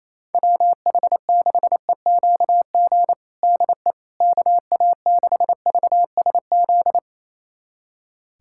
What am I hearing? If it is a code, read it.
Morse code, heard as a keyed tone, about 28 wpm, 700 Hz. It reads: WH6EQG DE KA6VSZ
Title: USA Callsign DE Callsign - 6 Characters - 28wpm